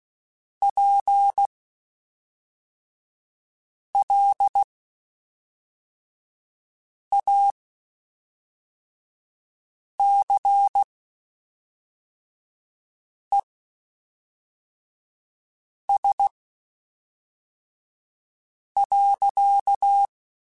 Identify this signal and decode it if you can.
Morse code, heard as a keyed tone, about 16 wpm, spaced as for 3 wpm, 800 Hz. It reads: PLACES.
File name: Spaced / Chuckles Spaced-mp3s / MiddleAge sp @5